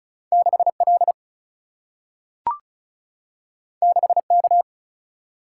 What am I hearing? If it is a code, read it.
Morse code, heard as a keyed tone, about 36 wpm, 700 Hz. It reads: 6 L 6 K
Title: Single-Letter Single-Number 35wpm